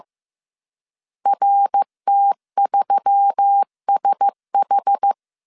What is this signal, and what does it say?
Morse code, heard as a keyed tone, about 15 wpm, 800 Hz. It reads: RT3SH
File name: CW / DET / 1 6 1500C S A 14wpm 2921